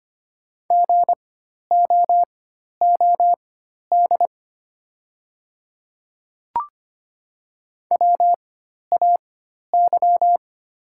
Morse code, heard as a keyed tone, about 25 wpm, spaced as for 12 wpm, 690 Hz. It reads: GOOD WAY
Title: 4x Character Spacing - Top 200 Words - 25wpm